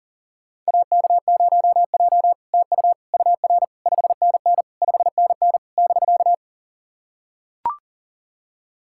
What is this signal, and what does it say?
Morse code, heard as a keyed tone, about 40 wpm, 700 Hz. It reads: AK0J TU UR 5NN 5NN <BK>
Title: SOTA QSO Practice 40wpm